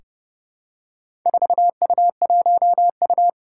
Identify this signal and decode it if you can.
Morse code, heard as a keyed tone, about 30 wpm, 700 Hz. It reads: 4U1U
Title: Four Challenging Letter-Number Combinations 30wpm